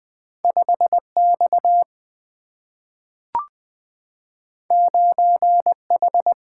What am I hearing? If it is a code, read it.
Morse code, heard as a keyed tone, about 20 wpm, 700 Hz. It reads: 5 X 9 H
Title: Single-Letter Single-Number 20wpm